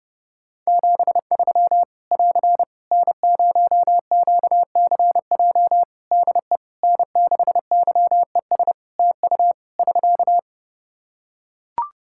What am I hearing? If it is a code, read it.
Morse code, heard as a keyed tone, about 30 wpm, 700 Hz. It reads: 73 <AR> N0QCJ DE N6YES TU <SK>
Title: K1USN SST QSO Practice 30wpm